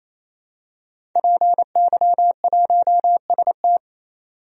Morse code, heard as a keyed tone, about 28 wpm, 700 Hz. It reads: PY1ST
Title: DXCC-100 Call Signs - 5 Characters - 28wpm